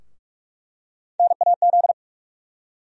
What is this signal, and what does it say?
Morse code, heard as a keyed tone, about 46 wpm, 700 Hz. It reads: NAZ
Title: USA Call Sign Postfixes - 3 Letters 45wpm